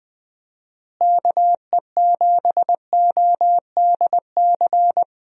Morse code, heard as a keyed tone, about 20 wpm, 700 Hz. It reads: KE7ODC
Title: USA Call Signs - 6 Characters - Repeated 3x - 20wpm